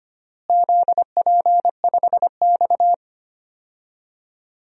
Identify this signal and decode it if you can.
Morse code, heard as a keyed tone, about 25 wpm, 700 Hz. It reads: ZP5X
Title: DXCC-100 Call Signs - 4 Characters - Repeated 2x - 25wpm